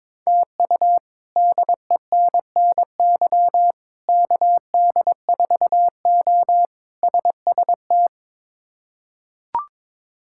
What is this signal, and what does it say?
Morse code, heard as a keyed tone, about 22 wpm, 700 Hz. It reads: TU DENNY KD4O SST
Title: K1USN SST QSO Practice - 22wpm